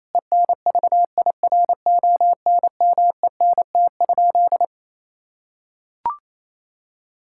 Morse code, heard as a keyed tone, about 28 wpm, 700 Hz. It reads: ENVIRONMENT?
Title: Sentences from Top 1000 Words - Review Part 5 - 28wpm